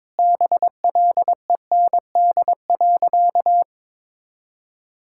Morse code, heard as a keyed tone, about 21 wpm, 700 Hz. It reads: BLEND.
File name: practice / 22wpm / Sentences with 4-Word Phrases - 22wpm